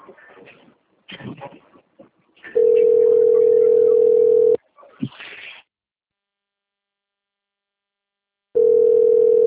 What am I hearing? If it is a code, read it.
Ringback tone (North America), 440 + 480 Hz.